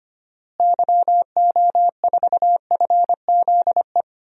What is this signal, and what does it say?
Morse code, heard as a keyed tone, about 25 wpm, 700 Hz. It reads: YO4FZE
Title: DXCC-100 Call Signs - 6 Characters - Repeated 2x - 25wpm